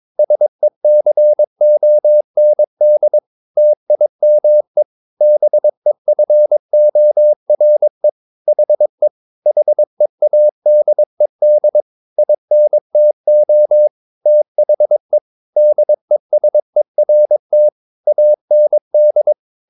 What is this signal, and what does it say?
Morse code, heard as a keyed tone, about 22 wpm, 600 Hz. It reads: SECOND TIME BEFORE HE HEADED INTO THE DESERT AND
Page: Romanen Never af Ken Follet på engelsk.